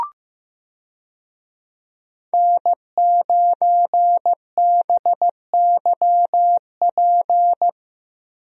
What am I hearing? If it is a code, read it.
Morse code, heard as a keyed tone, about 15 wpm, 700 Hz. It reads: N9BYP
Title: USA Call Signs - 5 Characters - Repeated 2x - 15wpm